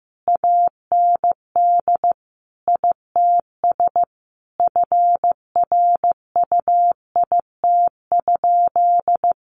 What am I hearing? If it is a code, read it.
Morse code, heard as a keyed tone, about 15 wpm, 700 Hz. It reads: AND ITS FRUIT?
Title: Sentences from Top 1500 Words 15wpm